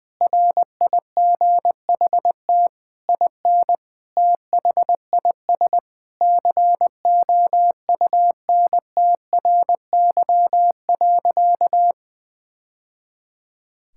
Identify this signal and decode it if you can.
Morse code, heard as a keyed tone, about 21 wpm, 700 Hz. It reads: RIGHT IN THIS COUNTRY.
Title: Sentences from Top 1000 Words - Review Part 5 - 20wpm